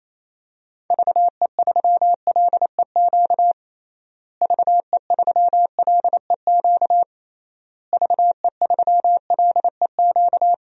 Morse code, heard as a keyed tone, about 28 wpm, 700 Hz. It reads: VE3LEQ VE3LEQ VE3LEQ
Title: DXCC-100 Call Signs - 6 Characters - Repeated 3x - 28wpm